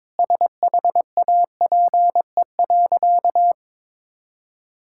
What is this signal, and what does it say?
Morse code, heard as a keyed tone, about 22 wpm, 700 Hz. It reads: SHAPE.
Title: Sentences from Top 1000 Words - Review Part 6 - 22wpm